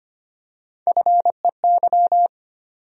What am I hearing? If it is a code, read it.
Morse code, heard as a keyed tone, about 25 wpm, 700 Hz. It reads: FEY
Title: USA Call Sign Postfixes - 3 Letters 25wpm